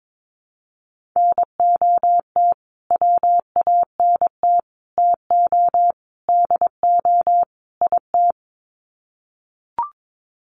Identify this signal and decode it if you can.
Morse code, heard as a keyed tone, about 22 wpm, 700 Hz. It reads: NOT WANT TO DO IT
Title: Sets of 5 Words Encore 22wpm